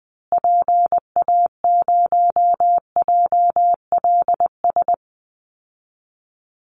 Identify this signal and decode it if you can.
Morse code, heard as a keyed tone, about 20 wpm, 700 Hz. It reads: PA0JLS
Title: Call Signs Repeated 3x - 20wpm